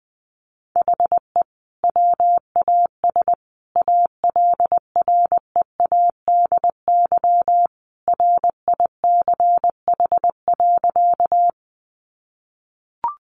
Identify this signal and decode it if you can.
Morse code, heard as a keyed tone, about 20 wpm, 700 Hz. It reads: HE WAS ALREADY RICH.
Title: Sentences from Top 1100 Words 20wpm